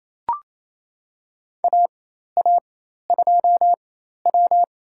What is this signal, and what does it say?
Morse code, heard as a keyed tone, about 28 wpm, spaced as for 13 wpm, 700 Hz. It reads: AA2W
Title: USA Call Signs - 4x Character Spacing - 28wpm